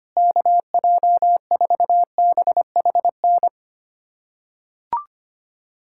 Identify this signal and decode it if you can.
Morse code, heard as a keyed tone, about 25 wpm, 700 Hz. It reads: KJ4BHN
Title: USA Callsign DE Callsign 25wpm